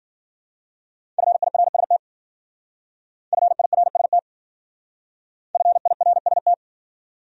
Morse code, heard as a keyed tone, about 50 wpm, 700 Hz. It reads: FIRST FIRST FIRST
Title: Speed-Racing Top 200 Words 40wpm